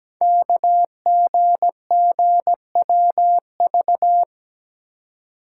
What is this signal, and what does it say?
Morse code, heard as a keyed tone, about 17 wpm, 700 Hz. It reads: KGGWV
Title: Code Groups - 5 Letters - 17wpm